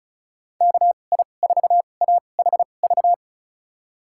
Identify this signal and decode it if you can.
Morse code, heard as a keyed tone, about 36 wpm, spaced as for 25 wpm, 700 Hz. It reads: KI4AHV
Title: USA Call Signs - 6 Characters - 2x Character Spacing - 35wpm